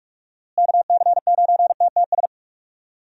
Morse code, heard as a keyed tone, about 41 wpm, 700 Hz. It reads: KC9TTS
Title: USA Call Signs - 6 Characters - Repeated 3x - 45wpm